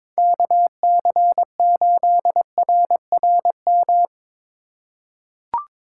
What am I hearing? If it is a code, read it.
Morse code, heard as a keyed tone, about 21 wpm, 700 Hz. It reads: KC8RRM
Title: USA Call Signs - 6 Characters - 2x Character Spacing - 22wpm